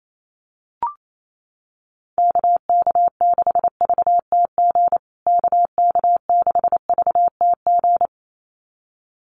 Morse code, heard as a keyed tone, about 28 wpm, 700 Hz. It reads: KK6VTG KK6VTG
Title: USA Call Signs Repeated 2x 28wpm